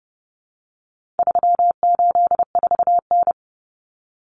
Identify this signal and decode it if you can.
Morse code, heard as a keyed tone, about 30 wpm, 700 Hz. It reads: 384N
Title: Common Cut Numbers - 4-Digit 30wpm